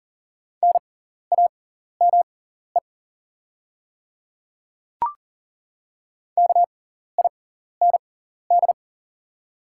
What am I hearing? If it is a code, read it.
Morse code, heard as a keyed tone, about 40 wpm, spaced as for 14 wpm, 700 Hz. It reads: NAME KIND
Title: 6x Character Spacing - Top 300 Words - 40wpm